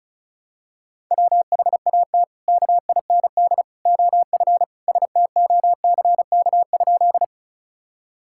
Morse code, heard as a keyed tone, about 36 wpm, 700 Hz. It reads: WHAT KIND OF STOCK?
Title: Sentences from Top 1000 Words - Review Part 5 - 35wpm